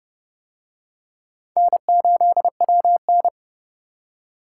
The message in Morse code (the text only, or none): N8WN